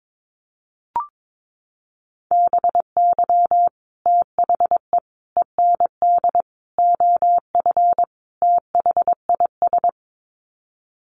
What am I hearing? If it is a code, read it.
Morse code, heard as a keyed tone, about 22 wpm, 700 Hz. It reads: BY THE END OF THIS
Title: Sets of 5 Words Encore 22wpm